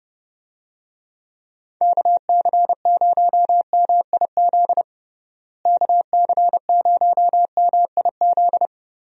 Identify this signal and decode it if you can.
Morse code, heard as a keyed tone, about 32 wpm, 700 Hz. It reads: KC0MIZ KC0MIZ
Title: USA Call Signs - 6 Characters - Repeated 2x - 30wpm